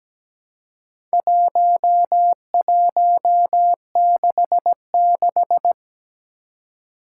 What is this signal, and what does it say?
Morse code, heard as a keyed tone, about 17 wpm, 700 Hz. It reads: 1166
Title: Four Challenging Letter-Number Combinations 17wpm